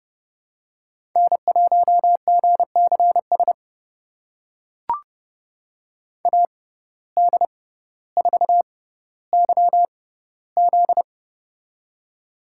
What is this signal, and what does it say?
Morse code, heard as a keyed tone, about 30 wpm, spaced as for 15 wpm, 700 Hz. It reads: N1GCS AD4YZ
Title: USA Call Signs - 5 Characters - 6x Character Spacing - 30wpm